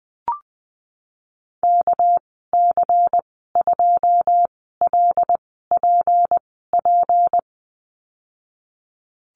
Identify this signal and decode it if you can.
Morse code, heard as a keyed tone, about 20 wpm, spaced as for 15 wpm, 700 Hz. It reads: KC2LPP
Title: 238 Lesson 38g - Letter q - Call Signs 20wpm